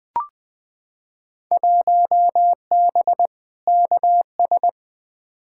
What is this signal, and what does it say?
Morse code, heard as a keyed tone, about 21 wpm, 700 Hz. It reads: 1B KS
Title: Field Day Exchanges Encore - 20wpm